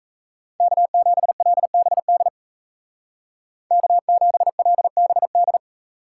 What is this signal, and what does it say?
Morse code, heard as a keyed tone, about 37 wpm, 700 Hz. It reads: K7LBD K7LBD